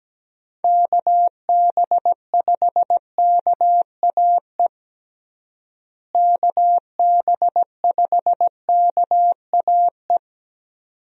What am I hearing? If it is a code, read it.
Morse code, heard as a keyed tone, about 17 wpm, 700 Hz. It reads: KB5KAE KB5KAE
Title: USA Call Signs - 6 Characters - Repeated 2x - 17wpm